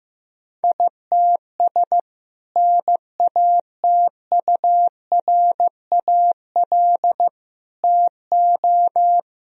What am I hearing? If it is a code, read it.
Morse code, heard as a keyed tone, about 15 wpm, 700 Hz. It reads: ITS NATURAL TO
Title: Sentences from Top 1000 Words - Review Part 4 - 15wpm